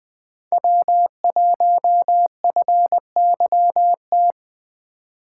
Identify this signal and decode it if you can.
Morse code, heard as a keyed tone, about 20 wpm, 700 Hz. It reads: W1FYT